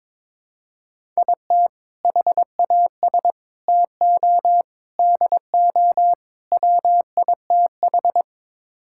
Morse code, heard as a keyed tone, about 21 wpm, 700 Hz. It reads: IT HAS TO DO WITH
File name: practice / 22wpm / Build A Phrase - 5-Word Phrases - 22wpm